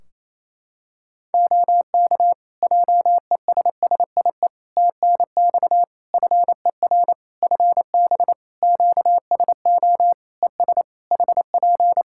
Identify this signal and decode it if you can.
Morse code, heard as a keyed tone, about 28 wpm, 700 Hz. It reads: OK JESSIE TNX FER FB QSO ES HP
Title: Practice QSO - Exchange 4 - Ending - 28wpm